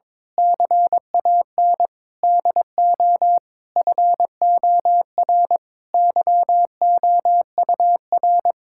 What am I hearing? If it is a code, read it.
Morse code, heard as a keyed tone, about 22 wpm, 700 Hz. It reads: CAN DO FOR YOUR
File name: Sentences with 9-Word Phrases - 22wpm